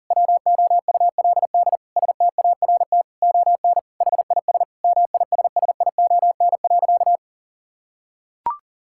Morse code, heard as a keyed tone, about 40 wpm, 700 Hz. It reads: WOULD START ON HIS MISSION.
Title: Sentences from Top 1600 Words 40wpm